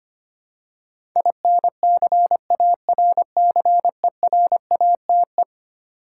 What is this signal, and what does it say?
Morse code, heard as a keyed tone, about 25 wpm, 700 Hz. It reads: INCARCERATE
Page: morse-code-ninja-practice-25wpm